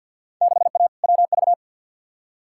Morse code, heard as a keyed tone, about 48 wpm, 700 Hz. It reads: 6A WV
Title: Field Day Exchanges - 50wpm